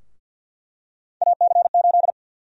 Morse code, heard as a keyed tone, about 48 wpm, 700 Hz. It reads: AC8
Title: USA Call Sign Prefixes - 50wpm